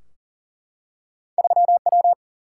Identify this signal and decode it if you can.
Morse code, heard as a keyed tone, about 38 wpm, 700 Hz. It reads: 3W
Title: Single-Letter Single-Number 40wpm